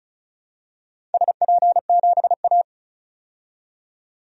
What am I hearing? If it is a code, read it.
Morse code, heard as a keyed tone, about 36 wpm, 690 Hz. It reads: SP7A
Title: DXCC-100 Call Signs - 4 Characters - 35wpm